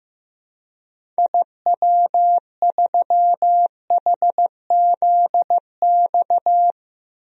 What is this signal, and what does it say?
Morse code, heard as a keyed tone, about 16 wpm, 700 Hz. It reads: IW3HZX
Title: DXCC-100 Call Signs - 6 Characters - 15wpm